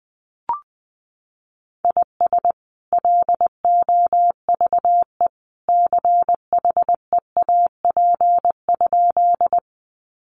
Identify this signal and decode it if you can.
Morse code, heard as a keyed tone, about 20 wpm, 700 Hz. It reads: IS LOVE CHEAP?
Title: Sentences from Top 1200 Words 20wpm